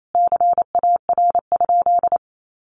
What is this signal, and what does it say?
Morse code, heard as a keyed tone, about 28 wpm, 700 Hz. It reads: CAR?
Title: Sentences from Top 300 Words 28wpm